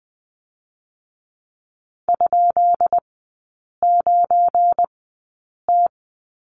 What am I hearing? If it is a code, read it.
Morse code, heard as a keyed tone, about 20 wpm, 700 Hz. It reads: ? 9 T
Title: 198 Lesson 32e - Letter k - Three Characters 20wpm